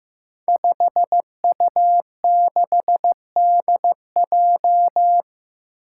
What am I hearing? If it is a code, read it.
Morse code, heard as a keyed tone, about 15 wpm, 700 Hz. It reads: 5U6DJ
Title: Five Challenging Letter-Number Combinations 15wpm